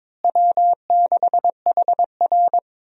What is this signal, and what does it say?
Morse code, heard as a keyed tone, about 22 wpm, 700 Hz. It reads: W6HR
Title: USA Call Signs - 4 Characters - Repeated 3x - 22wpm